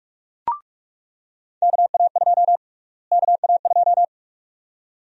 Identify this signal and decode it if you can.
Morse code, heard as a keyed tone, about 42 wpm, 700 Hz. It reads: KA2 KA2
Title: USA Call Sign Prefixes - Repeated 2x - 45wpm